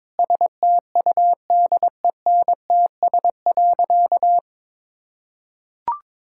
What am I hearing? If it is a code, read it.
Morse code, heard as a keyed tone, about 22 wpm, 700 Hz. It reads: STUDENTS.
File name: practice / 22wpm / Sentences with 4-Word Phrases - 22wpm